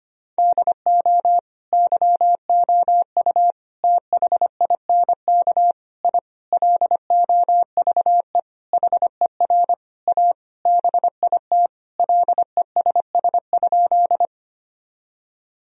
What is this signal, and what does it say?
Morse code, heard as a keyed tone, about 25 wpm, 700 Hz. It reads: DO YOU THINK I LOVE HER A BIT LESS?
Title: Sentences from Top 1000 Words - Review Part 2 - 25wpm